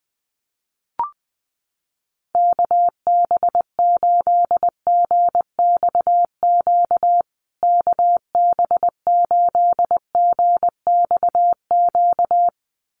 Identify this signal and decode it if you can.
Morse code, heard as a keyed tone, about 20 wpm, 700 Hz. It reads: KB8GXQ KB8GXQ
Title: USA Call Signs Repeated 2x 20wpm